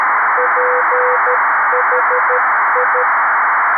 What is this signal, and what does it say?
Morse code, heard as a keyed tone, about 13 wpm, 500 Hz, with noise. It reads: PHI
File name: arissat / recordings / ARISSat 11Sept2011 1359utc 2mins 14frames